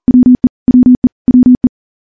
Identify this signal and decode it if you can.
Morse code, heard as a keyed tone, about 42 wpm, 250 Hz. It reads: WE WE WE
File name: morse / morse-training-40WPM